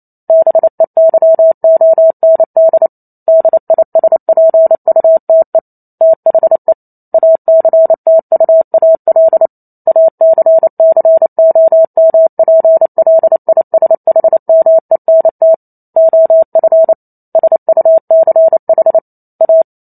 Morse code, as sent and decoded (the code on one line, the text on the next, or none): -... -.-- --- -. -.. / -.. .. ... .--. ..- - . / - .... / .- -.-. - ..- .- .-.. / .- -.-. -.-. --- -- .--. .-.. .. ... .... -- . -. - / --- ..-. / ... ..- -.-. .... / .-
BEYOND DISPUTE THE ACTUAL ACCOMPLISHMENT OF SUCH A